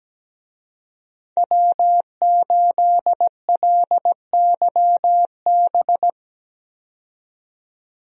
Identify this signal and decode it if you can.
Morse code, heard as a keyed tone, about 17 wpm, 700 Hz. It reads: W8LYB
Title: USA Call Signs - 5 Characters - 17wpm